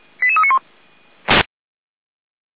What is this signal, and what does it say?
Five-tone selective call (ZVEI), heard as quick runs of five tones, tones of about 70 ms. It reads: 88391
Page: SDR/ZVEI